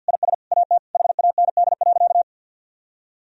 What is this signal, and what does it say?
Morse code, heard as a keyed tone, about 51 wpm, 700 Hz. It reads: IS AT HAND.
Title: Sentences from Top 1000 Words - Review Part 4 - 50wpm